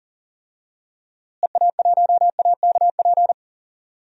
Morse code, heard as a keyed tone, about 38 wpm, 700 Hz. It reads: EA1AKP
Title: DXCC-100 Call Signs - 6 Characters - Repeated 3x - 40wpm